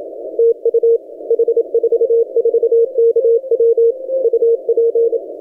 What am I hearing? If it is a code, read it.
Morse code, heard as a keyed tone, about 27 wpm, 460 Hz. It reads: TU H44KW UP
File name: H44KW 17 CW